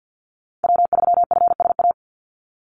Morse code, heard as a keyed tone, about 50 wpm, 700 Hz. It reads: W3FSA
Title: Call Signs - 50wpm